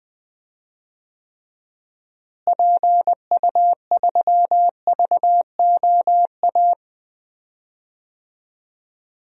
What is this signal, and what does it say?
Morse code, heard as a keyed tone, about 20 wpm, 700 Hz. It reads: PU3VOA
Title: DXCC-100 Call Signs - 6 Characters - Repeated 3x - 20wpm